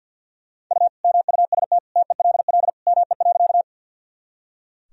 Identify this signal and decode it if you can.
Morse code, heard as a keyed tone, about 46 wpm, 700 Hz. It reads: U MUST TELL ME.
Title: Sentences from Top 200 Words and Abbreviations 50wpm